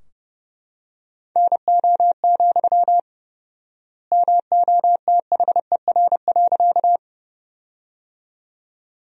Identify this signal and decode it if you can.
Morse code, heard as a keyed tone, about 32 wpm, 700 Hz. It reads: NO, MOTHER.
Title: 4x Word-Spacing - Sentences from Top 200 Words - 30wpm